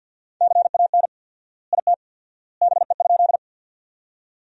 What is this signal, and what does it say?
Morse code, heard as a keyed tone, about 50 wpm, 700 Hz. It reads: CAN IT BE?
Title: 4x Word-Spacing - Sentences from Top 100 Words - 50wpm